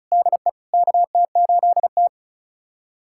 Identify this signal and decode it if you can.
Morse code, heard as a keyed tone, about 36 wpm, 700 Hz. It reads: DE KT8T
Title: USA Callsign DE Callsign - 4 Characters - 35wpm